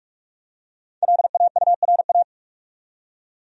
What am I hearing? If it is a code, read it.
Morse code, heard as a keyed tone, about 43 wpm, 700 Hz. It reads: LAURA
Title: USA Common Names - Max 5 Letters - 45wpm